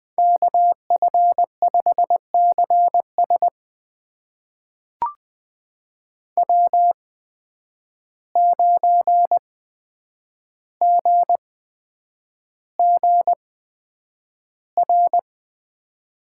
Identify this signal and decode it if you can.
Morse code, heard as a keyed tone, about 20 wpm, spaced as for 9 wpm, 700 Hz. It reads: KF5CS W9GGR
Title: USA Call Signs - 5 Characters - 8x Character Spacing - 20wpm